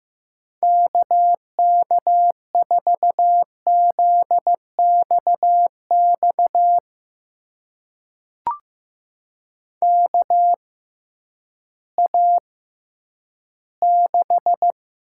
Morse code, heard as a keyed tone, about 15 wpm, spaced as for 10 wpm, 700 Hz. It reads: KK4ZXX KA6
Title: USA Call Signs - 6 Characters - 6x Character Spacing - 15wpm